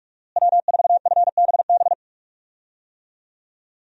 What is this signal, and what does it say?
Morse code, heard as a keyed tone, about 41 wpm, 700 Hz. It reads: W4FBB
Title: USA Call Signs - 5 Characters - Repeated 2x - 45wpm